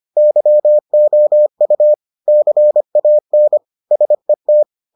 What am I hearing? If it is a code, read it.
Morse code, heard as a keyed tone, about 25 wpm, 600 Hz. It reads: YOU CAN SET